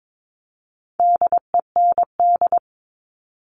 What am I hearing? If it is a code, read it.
Morse code, heard as a keyed tone, about 22 wpm, 700 Hz. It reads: DEND
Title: All Cut Numbers - 4-Digit 22wpm